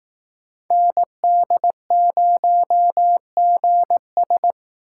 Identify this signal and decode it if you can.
Morse code, heard as a keyed tone, about 18 wpm, 700 Hz. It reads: ND0GS
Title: Speed Racing - USA Call Signs 15wpm